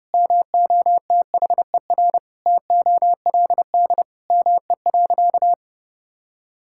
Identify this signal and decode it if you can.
Morse code, heard as a keyed tone, about 30 wpm, 700 Hz. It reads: MOTHER TOLD ME.
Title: Sentences from Top 1000 Words - Review Part 6 - 30wpm